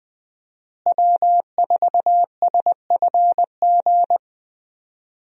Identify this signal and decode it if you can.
Morse code, heard as a keyed tone, about 20 wpm, 700 Hz. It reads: W4SFG
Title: USA Call Signs - 5 Characters - Repeated 2x - 20wpm